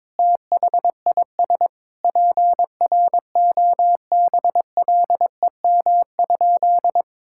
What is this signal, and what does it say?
Morse code, heard as a keyed tone, about 22 wpm, 700 Hz. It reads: THIS PROBLEM?
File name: practice / 22wpm / Sentences with 8-Word Phrases - 22wpm